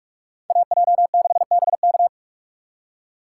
Morse code, heard as a keyed tone, about 48 wpm, 700 Hz. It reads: AJ6BK